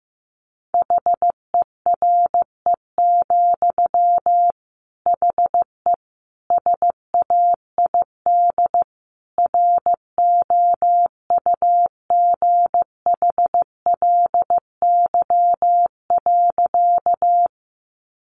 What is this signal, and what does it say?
Morse code, heard as a keyed tone, about 15 wpm, 700 Hz. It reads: HERE, HE SAID ROUGHLY.
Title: Sentences from Top 1500 Words 15wpm